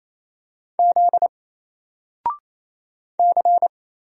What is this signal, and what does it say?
Morse code, heard as a keyed tone, about 27 wpm, 700 Hz. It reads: Z C
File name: Single-Letter-Number-28wpm